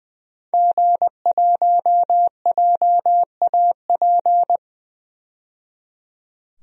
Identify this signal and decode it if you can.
Morse code, heard as a keyed tone, about 21 wpm, 700 Hz. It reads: G1JAP
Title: DXCC-100 Call Signs - 5 Characters - 20wpm